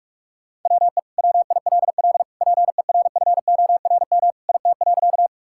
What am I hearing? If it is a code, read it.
Morse code, heard as a keyed tone, about 46 wpm, 700 Hz. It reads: WE WILL PERFORM IT.
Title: Sentences from Top 1000 Words - Review Part 5 - 45wpm